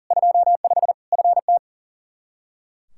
Morse code, heard as a keyed tone, about 40 wpm, 700 Hz. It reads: 25 FT
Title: Practice QSO - Exchange 2 - Info - 40wpm